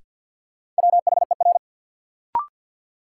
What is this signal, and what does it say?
Morse code, heard as a keyed tone, about 53 wpm, 700 Hz. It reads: WHER
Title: Sentences from Top 1500 Words - New Word Drill - 50wpm